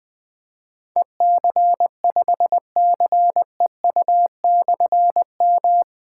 Morse code, heard as a keyed tone, about 20 wpm, 700 Hz. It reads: EC5CEU/M
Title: Difficult Call Signs 20wpm